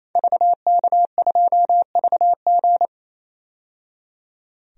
Morse code, heard as a keyed tone, about 28 wpm, 700 Hz. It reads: VK2VG
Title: DXCC-100 Call Signs - 5 Characters - 28wpm